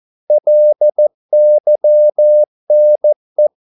Morse code, heard as a keyed tone, about 14 wpm, 600 Hz. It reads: LYNE
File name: Korte ord DK 14wpm